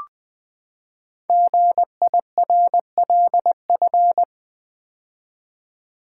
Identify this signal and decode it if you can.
Morse code, heard as a keyed tone, about 20 wpm, 700 Hz. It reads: GIRLF
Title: Build A Word - 5 Letters At A Time - 10-Letter Words - 20wpm